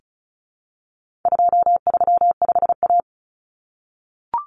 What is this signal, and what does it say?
Morse code, heard as a keyed tone, about 34 wpm, 700 Hz. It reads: 235A
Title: Common Cut Numbers - 4-Digit 35wpm